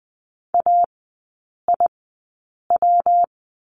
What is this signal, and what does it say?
Morse code, heard as a keyed tone, about 19 wpm, spaced as for 8 wpm, 700 Hz. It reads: AIW
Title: 133 Lesson 22e - Question Mark - Three Characters 20wpm